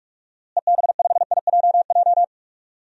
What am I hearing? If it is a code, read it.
Morse code, heard as a keyed tone, about 42 wpm, 700 Hz. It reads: EB5IJJ
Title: DXCC-100 Call Signs - 6 Characters - Repeated 2x - 45wpm